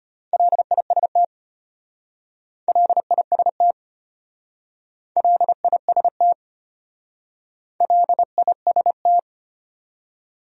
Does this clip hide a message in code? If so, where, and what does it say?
Yes: Morse code from 0.3 to 9.2 s, heard as a keyed tone, about 32 wpm, 700 Hz. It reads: LIST LIST LIST LIST